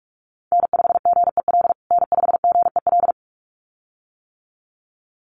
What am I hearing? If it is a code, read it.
Morse code, heard as a keyed tone, about 44 wpm, 700 Hz. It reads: N5GEL N5GEL
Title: USA Call Signs Repeated 2x 45wpm